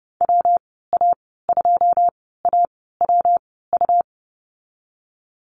Morse code, heard as a keyed tone, about 32 wpm, spaced as for 17 wpm, 700 Hz. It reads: WA2AWU
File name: Lesson 37g - Letter x - Call Signs 30wpm